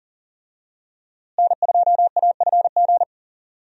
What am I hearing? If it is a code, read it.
Morse code, heard as a keyed tone, about 40 wpm, 700 Hz. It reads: N2AFG
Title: USA Call Signs - 5 Characters - Repeated 3x - 40wpm